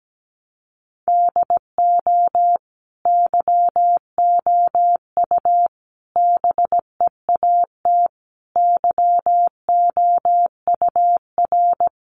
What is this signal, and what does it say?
Morse code, heard as a keyed tone, about 17 wpm, 700 Hz. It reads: DO YOU BEAT YOUR
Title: Sentences from Top 1300 Words 17wpm